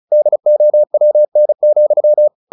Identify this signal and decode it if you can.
Morse code, heard as a keyed tone, about 34 wpm, 600 Hz. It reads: DOWN,
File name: the old man and the sea 2 35wpm